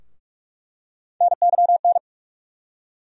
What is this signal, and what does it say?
Morse code, heard as a keyed tone, about 47 wpm, 700 Hz. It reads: NYN
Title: USA Call Sign Postfixes - 3 Letters 45wpm